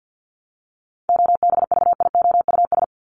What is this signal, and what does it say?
Morse code, heard as a keyed tone, about 50 wpm, 700 Hz. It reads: OBVIOUS
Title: Top 1501-1750 Words 50wpm